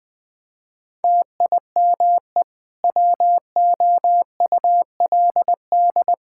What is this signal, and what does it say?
Morse code, heard as a keyed tone, about 20 wpm, 700 Hz. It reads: TIME WOULD
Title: Sentences from Top 1000 Words - Review Part 5 - 20wpm